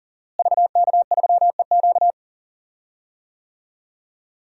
Morse code, heard as a keyed tone, about 40 wpm, 700 Hz. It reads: VK3EQ
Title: DXCC-100 Call Signs - 5 Characters - Repeated 3x - 40wpm